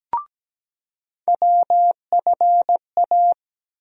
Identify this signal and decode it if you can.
Morse code, heard as a keyed tone, about 17 wpm, 700 Hz. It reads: WFA
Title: USA Call Sign Postfixes - 3 Letters 17wpm